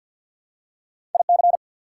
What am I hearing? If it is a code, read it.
Morse code, heard as a keyed tone, about 49 wpm, 690 Hz. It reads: IC
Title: Prefixes and Suffixes 50wpm